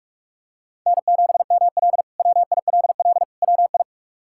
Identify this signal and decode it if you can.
Morse code, heard as a keyed tone, about 46 wpm, 700 Hz. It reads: N7ML WILL WI
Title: K1USN SST QSO Practice - 45wpm